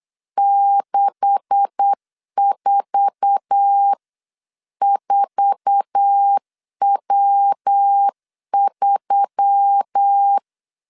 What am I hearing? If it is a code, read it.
Morse code, heard as a keyed tone, about 8 wpm, 800 Hz. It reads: 644W3
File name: CW / DET / 1 9 1500C S B 8wpm 5624